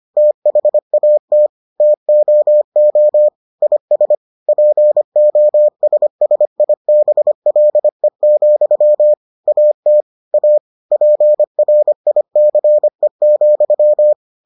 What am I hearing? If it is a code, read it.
Morse code, heard as a keyed tone, about 25 wpm, 600 Hz. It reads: THAT TOO IS POSSIBLE, AT A PRICE,